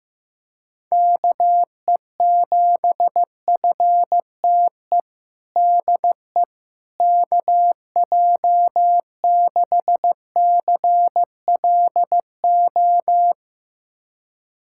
Morse code, heard as a keyed tone, about 15 wpm, 700 Hz. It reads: KE7FTE DE KJ6CLO
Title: USA Callsign DE Callsign - 6 Characters - 15wpm